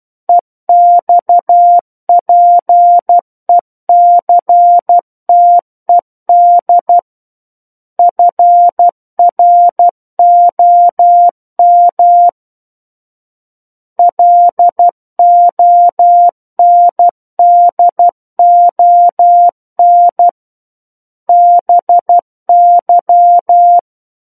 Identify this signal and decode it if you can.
Morse code, heard as a keyed tone, about 12 wpm, 700 Hz. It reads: EXPECTED FROM LONDON BY
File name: Book One / 03-Chapter 3 - 12 WPM